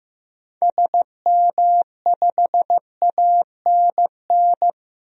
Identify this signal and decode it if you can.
Morse code, heard as a keyed tone, about 15 wpm, 700 Hz. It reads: SM5ANN
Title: DXCC-100 Call Signs - 6 Characters - 15wpm